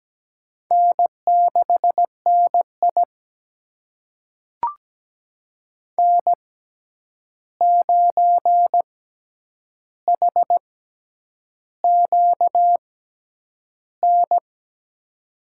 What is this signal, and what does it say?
Morse code, heard as a keyed tone, about 17 wpm, spaced as for 8 wpm, 700 Hz. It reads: N6NI N9HQN
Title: USA Call Signs - 6x Character Spacing - 17wpm